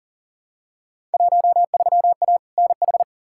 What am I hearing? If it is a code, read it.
Morse code, heard as a keyed tone, about 40 wpm, 700 Hz. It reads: 13A NH
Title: Field Day Exchanges Encore - 40wpm